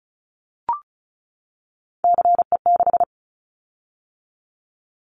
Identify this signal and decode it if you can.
Morse code, heard as a keyed tone, about 36 wpm, 700 Hz. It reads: CE6
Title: Callsign Prefixes for Top 100 DXCC - 35wpm